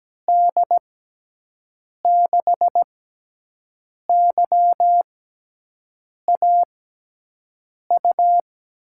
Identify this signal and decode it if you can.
Morse code, heard as a keyed tone, about 17 wpm, spaced as for 6 wpm, 700 Hz. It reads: D6YAU
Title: USA Call Signs - 6 Characters - 6x Character Spacing - 17wpm